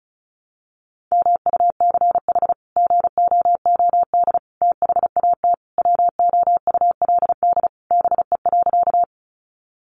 Morse code, heard as a keyed tone, about 34 wpm, 700 Hz. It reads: MUCH GOOD THAT WOULD BE.
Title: Sentences from Top 200 Words 35wpm